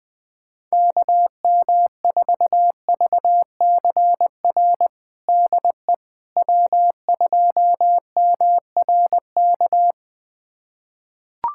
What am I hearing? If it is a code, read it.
Morse code, heard as a keyed tone, about 20 wpm, 700 Hz. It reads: KM4VCR DE W2MRK
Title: USA Callsign DE Callsign 20wpm